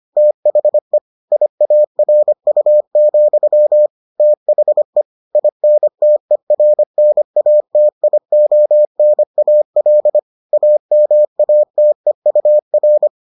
Morse code, heard as a keyed tone, about 25 wpm, 600 Hz. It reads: THE IARU, THE INTERNATIONAL AMATEUR